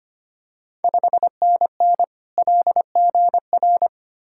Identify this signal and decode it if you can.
Morse code, heard as a keyed tone, about 25 wpm, 700 Hz. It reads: 5NN LGR
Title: Ontario QSO Party Exchange 25wpm